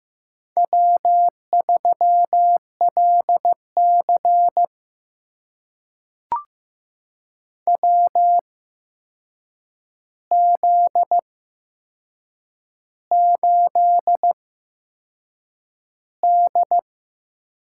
Morse code, heard as a keyed tone, about 15 wpm, spaced as for 6 wpm, 700 Hz. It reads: W3LC WZ8D
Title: USA Call Signs - 4 Characters - 8x Character Spacing - 15wpm